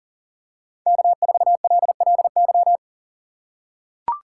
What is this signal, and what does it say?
Morse code, heard as a keyed tone, about 37 wpm, 700 Hz. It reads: K4LLY
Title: USA Call Signs - 5 Characters - 4x Character Spacing - 40wpm